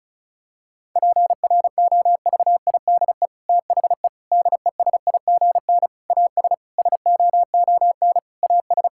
Morse code, heard as a keyed tone, about 34 wpm, 700 Hz. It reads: PROVIDE THE DESIGN AS SOON AS
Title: Sentences from Top 1000 Words - Review Part 5 - 35wpm